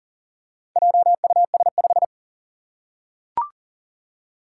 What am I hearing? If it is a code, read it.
Morse code, heard as a keyed tone, about 40 wpm, 700 Hz. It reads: JUS5
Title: Four Challenging Letter-Number Combinations 40wpm